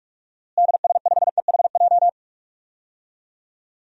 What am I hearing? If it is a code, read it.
Morse code, heard as a keyed tone, about 46 wpm, 700 Hz. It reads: DS5EHJ
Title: DXCC-100 Call Signs - 6 Characters - Repeated 2x - 45wpm